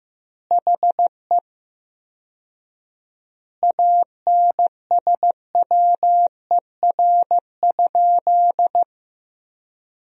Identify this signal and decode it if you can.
Morse code, heard as a keyed tone, about 16 wpm, 700 Hz. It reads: HE ANSWER?
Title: 4x Word-Spacing - Sentences from Top 300 Words - 15wpm